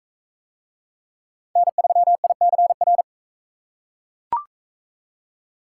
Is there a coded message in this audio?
Yes: Morse code from 1.6 to 3.0 s, heard as a keyed tone, about 41 wpm, 700 Hz. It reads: N3ICR